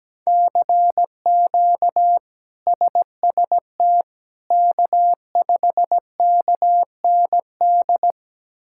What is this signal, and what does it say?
Morse code, heard as a keyed tone, about 17 wpm, 700 Hz. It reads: CQ SST K5KND
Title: K1USN SST QSO Practice - 17wpm